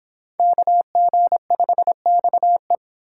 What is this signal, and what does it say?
Morse code, heard as a keyed tone, about 28 wpm, 700 Hz. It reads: KG5XE